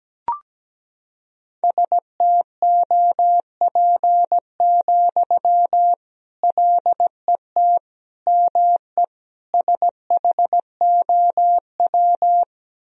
Morse code, heard as a keyed tone, about 17 wpm, 700 Hz. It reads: STOP, LET ME SHOW
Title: Sentences from Top 300 Words and Abbreviations 17wpm